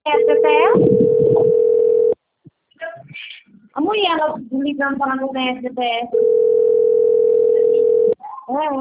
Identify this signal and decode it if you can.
Ringback tone (North America), 440 + 480 Hz, with speech.
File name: out- - - - - .58647